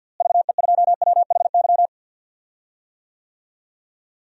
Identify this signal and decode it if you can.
Morse code, heard as a keyed tone, about 50 wpm, 700 Hz. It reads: VE2WHY
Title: DXCC-100 Call Signs - 6 Characters - Repeated 3x - 50wpm